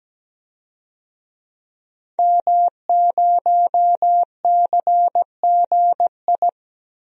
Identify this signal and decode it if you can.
Morse code, heard as a keyed tone, about 17 wpm, 700 Hz. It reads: M0CGI
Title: DXCC-100 Call Signs - 5 Characters - 17wpm